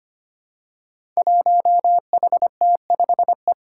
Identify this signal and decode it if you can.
Morse code, heard as a keyed tone, about 25 wpm, 700 Hz. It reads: 1HT5E
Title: Five Challenging Letter-Number Combinations 25wpm